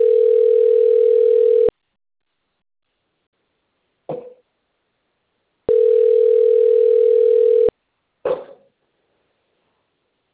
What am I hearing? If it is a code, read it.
Ringback tone (North America), 440 + 480 Hz.